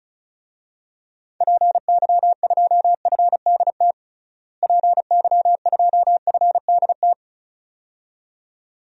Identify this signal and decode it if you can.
Morse code, heard as a keyed tone, about 36 wpm, 700 Hz. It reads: PY2FDT PY2FDT
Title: DXCC-100 Call Signs - 6 Characters - Repeated 2x - 35wpm